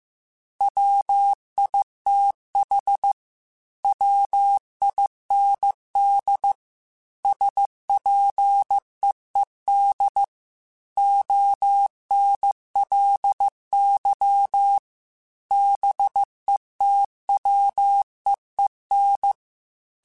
Morse code, heard as a keyed tone, about 15 wpm, 800 Hz. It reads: WITH WIND SPEED ONLY BETWEEN
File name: Weather Sets / Weather Sets-mp3s / Weather 5 1 @13